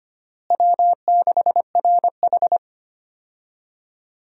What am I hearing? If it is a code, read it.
Morse code, heard as a keyed tone, about 25 wpm, 700 Hz. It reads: W6RH
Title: USA Call Signs - 4 Characters - Repeated 3x - 25wpm